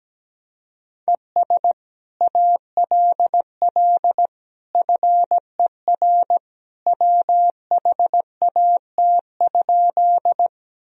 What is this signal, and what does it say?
Morse code, heard as a keyed tone, about 17 wpm, 700 Hz. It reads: ES ALL FER WHAT?
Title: Sentences from Top 100 Words and Abbreviations 17wpm